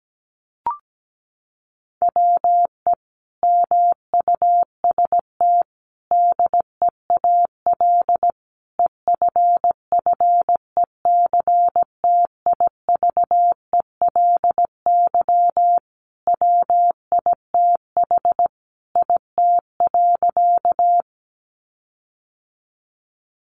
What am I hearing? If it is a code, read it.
Morse code, heard as a keyed tone, about 17 wpm, 700 Hz. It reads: WE MUST DEAL EFFECTIVELY WITH IT.
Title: Sentences from Top 1400 Words 17wpm